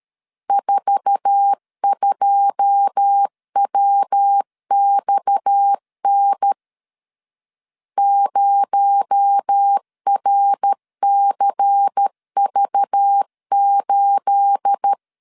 Morse code, heard as a keyed tone, about 13 wpm, 800 Hz. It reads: 42WXN 0RCV8
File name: LEZIONI / 10 SIGNS / 10A SLASH 12wpm 2314